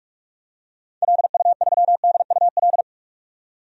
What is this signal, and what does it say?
Morse code, heard as a keyed tone, about 46 wpm, 700 Hz. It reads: LU3DUL
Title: DXCC-100 Call Signs - 6 Characters - 45wpm